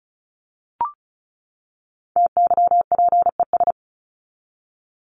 Morse code, heard as a keyed tone, about 34 wpm, 700 Hz. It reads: TYPES
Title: ES:S Suffix - Top 100 Words with Base Words 35wpm